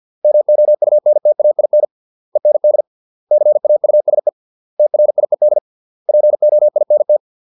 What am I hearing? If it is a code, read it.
Morse code, heard as a keyed tone, about 50 wpm, 600 Hz. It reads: MOUNTAIN END CAUSE TRIED POINT